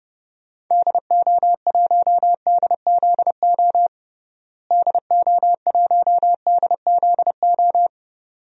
Morse code, heard as a keyed tone, about 30 wpm, 700 Hz. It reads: DO1DZO DO1DZO
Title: DXCC-100 Call Signs - 6 Characters - Repeated 2x - 30wpm